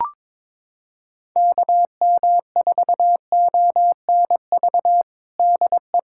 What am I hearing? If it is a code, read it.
Morse code, heard as a keyed tone, about 22 wpm, 700 Hz. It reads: KM4ONV DE
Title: USA Callsign DE Callsign - 6 Characters - 22wpm